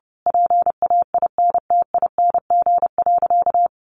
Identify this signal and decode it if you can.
Morse code, heard as a keyed tone, about 30 wpm, 700 Hz. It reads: PAINTING.
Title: Sentences from Top 1200 Words 30wpm